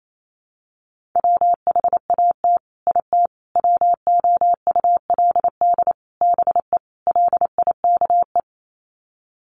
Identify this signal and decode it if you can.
Morse code, heard as a keyed tone, about 28 wpm, 700 Hz. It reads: WHAT IT WOULD BE LIKE
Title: Sets of 5 Words Encore 28wpm